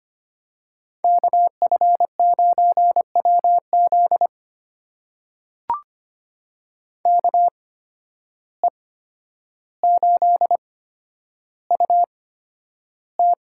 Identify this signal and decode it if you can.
Morse code, heard as a keyed tone, about 25 wpm, spaced as for 11 wpm, 700 Hz. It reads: KF9WZ KE8UT
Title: USA Call Signs - 5 Characters - 8x Character Spacing - 25wpm